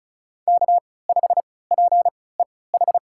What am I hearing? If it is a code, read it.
Morse code, heard as a keyed tone, about 34 wpm, spaced as for 20 wpm, 700 Hz. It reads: K5PEH
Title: USA Call Signs - 5 Characters - 3x Character Spacing - 35wpm